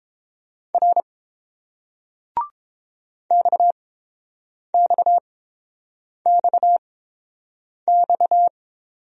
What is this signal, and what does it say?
Morse code, heard as a keyed tone, about 30 wpm, spaced as for 7 wpm, 700 Hz. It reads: R XXXX